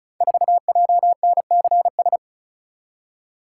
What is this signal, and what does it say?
Morse code, heard as a keyed tone, about 36 wpm, 700 Hz. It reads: 4JNCS
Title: Code Groups - 5 Letter:Numbers - 35wpm